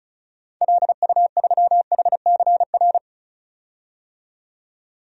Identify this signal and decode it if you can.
Morse code, heard as a keyed tone, about 36 wpm, 700 Hz. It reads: LU3HCR
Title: DXCC-100 Call Signs - 6 Characters - 35wpm